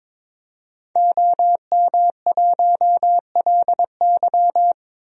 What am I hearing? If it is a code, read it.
Morse code, heard as a keyed tone, about 22 wpm, 700 Hz. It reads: OM1LY
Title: DXCC-100 Call Signs - 5 Characters - Repeated 3x - 22wpm